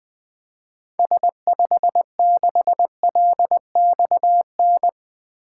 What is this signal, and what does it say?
Morse code, heard as a keyed tone, about 20 wpm, 700 Hz. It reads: S56LXN
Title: 2023 SOTA Activator Call Signs - 20wpm